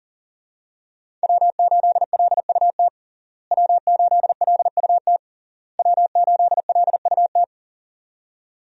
Morse code, heard as a keyed tone, about 40 wpm, 690 Hz. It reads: W8LUT W8LUT W8LUT
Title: USA Call Signs - 5 Characters - Repeated 3x - 40wpm